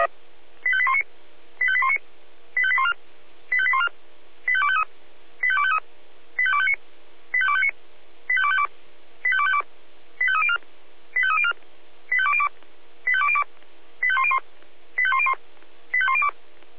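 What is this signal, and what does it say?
Five-tone selective call (ZVEI), heard as quick runs of five tones, tones of about 70 ms. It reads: 85819 85819 85814 85814 85253 85253 85259 85259 85252 85252 85304 85304 85202 85202 85101 85101 85102
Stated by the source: ZVEI five-tone selective call 85280 received by SDR.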